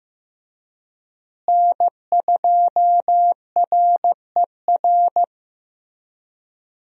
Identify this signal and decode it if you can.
Morse code, heard as a keyed tone, about 15 wpm, 700 Hz. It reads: N2RER
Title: USA Call Signs - 5 Characters - Repeated 2x - 15wpm